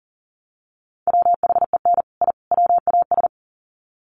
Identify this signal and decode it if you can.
Morse code, heard as a keyed tone, about 40 wpm, 700 Hz. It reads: WHEN I WAS
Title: Sets of 3 Words 40wpm